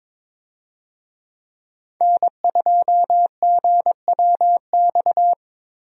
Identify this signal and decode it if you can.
Morse code, heard as a keyed tone, about 22 wpm, 700 Hz. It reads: N2GWX
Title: USA Call Signs - 5 Characters - Repeated 2x - 22wpm